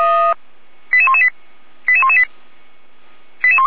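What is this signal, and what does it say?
Five-tone selective call (ZVEI), heard as quick runs of five tones, tones of about 70 ms. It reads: 88118 88118
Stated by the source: ZVEI five-tone selective call for 88288, received by SDR.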